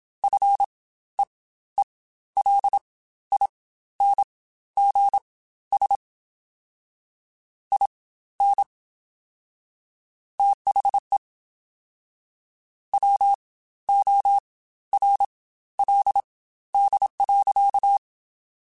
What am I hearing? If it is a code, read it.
Morse code, heard as a keyed tone, about 27 wpm, spaced as for 14 wpm, 800 Hz. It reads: FEELINGS IN THE WORLD.
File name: Quotes Spaced / Quotes Spaced-mp3s / ChildAsleep sp @25